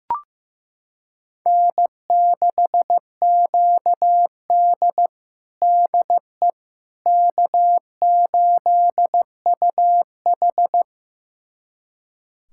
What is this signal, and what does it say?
Morse code, heard as a keyed tone, about 15 wpm, 700 Hz. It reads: N6QD DE K8UH
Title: USA Callsign DE Callsign - 4 Characters - 15wpm